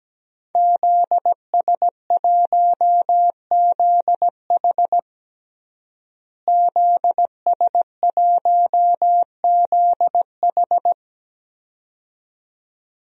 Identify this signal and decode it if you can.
Morse code, heard as a keyed tone, about 17 wpm, 700 Hz. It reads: ZS1ZH ZS1ZH
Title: DXCC-100 Call Signs - 5 Characters - Repeated 2x - 17wpm